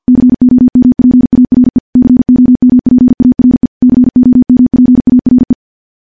Morse code, heard as a keyed tone, about 50 wpm, 250 Hz. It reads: COMPARE COMPARE COMPARE
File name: morse / morse-training-50WPM